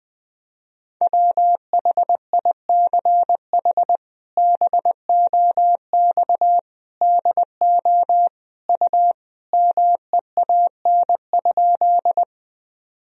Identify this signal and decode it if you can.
Morse code, heard as a keyed tone, about 20 wpm, 700 Hz. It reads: WHICH BOX DO U MEAN?
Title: Sentences from Top 500 Words and Abbreviations 20wpm